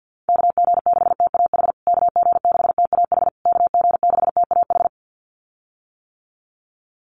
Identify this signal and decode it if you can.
Morse code, heard as a keyed tone, about 50 wpm, 700 Hz. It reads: KG6TAH KG6TAH KG6TAH
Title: USA Call Signs Repeated 3x 50wpm